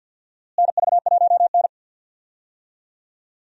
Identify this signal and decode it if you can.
Morse code, heard as a keyed tone, about 52 wpm, 700 Hz. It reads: NV1N
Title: USA Call Signs - 4 Characters - Repeated 3x - 50wpm